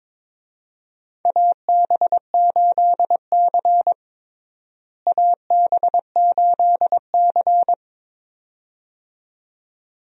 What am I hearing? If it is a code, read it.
Morse code, heard as a keyed tone, about 23 wpm, 700 Hz. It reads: AB8C AB8C
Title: USA Call Signs - 4 Characters - Repeated 2x - 22wpm